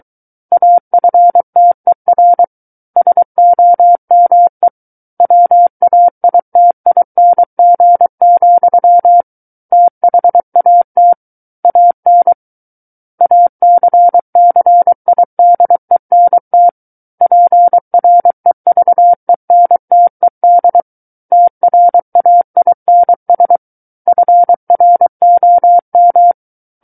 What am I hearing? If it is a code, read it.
Morse code, heard as a keyed tone, about 23 wpm, 700 Hz. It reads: AFTER SOME WAITING, THAT AN ACCIDENT PREVENTED TRAINS FROM
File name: Book One / 14-Chapter 14 - 23 WPM